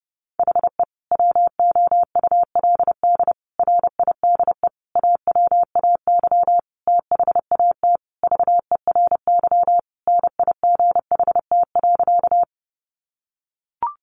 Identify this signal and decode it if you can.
Morse code, heard as a keyed tone, about 28 wpm, 700 Hz. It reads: HE WOULD RIDE AWAY THAT VERY NIGHT.
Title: Sentences from Top 1200 Words 30wpm